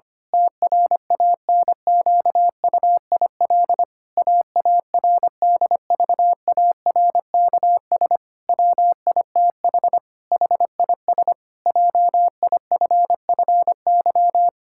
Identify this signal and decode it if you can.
Morse code, heard as a keyed tone, about 25 wpm, 700 Hz. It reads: TRANQUIL AARDVARKS WITH HIS JIFFY
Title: Pangram Challenge v2 - 25wpm